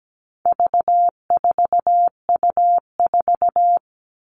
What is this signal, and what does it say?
Morse code, heard as a keyed tone, about 17 wpm, 700 Hz. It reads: V4U4
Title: All Cut Numbers - 4-Digit 17wpm